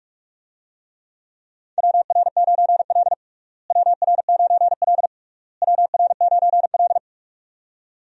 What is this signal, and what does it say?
Morse code, heard as a keyed tone, about 46 wpm, 700 Hz. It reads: WR9L WR9L WR9L
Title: USA Call Signs - 4 Characters - Repeated 3x - 45wpm